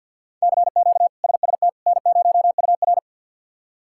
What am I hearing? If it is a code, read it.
Morse code, heard as a keyed tone, about 52 wpm, 700 Hz. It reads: CQ SST N0UR
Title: K1USN SST QSO Practice - 50wpm